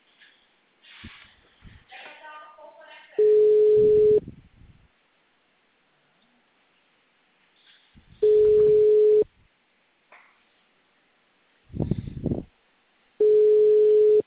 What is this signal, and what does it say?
Ringback tone (Europe, 425 Hz), with speech.